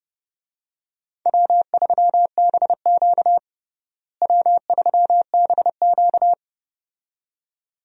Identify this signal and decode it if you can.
Morse code, heard as a keyed tone, about 32 wpm, 700 Hz. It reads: W3BQ W3BQ
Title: USA Call Signs - 4 Characters - Repeated 2x - 30wpm